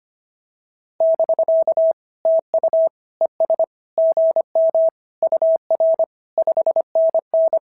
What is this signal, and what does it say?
Morse code, heard as a keyed tone, about 25 wpm, 650 Hz. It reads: <BK> TU ES GM UR 5NN
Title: SOTA QSO Practice 25wpm